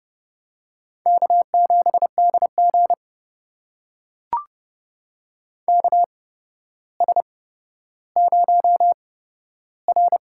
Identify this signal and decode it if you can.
Morse code, heard as a keyed tone, about 31 wpm, spaced as for 13 wpm, 700 Hz. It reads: K7DG KS0R
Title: USA Call Signs - 4 Characters - 8x Character Spacing - 30wpm